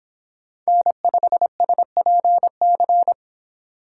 Morse code, heard as a keyed tone, about 25 wpm, 700 Hz. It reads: N5SPC